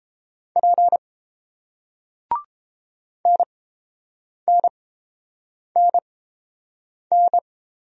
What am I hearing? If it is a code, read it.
Morse code, heard as a keyed tone, about 30 wpm, spaced as for 7 wpm, 700 Hz. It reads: P NNNN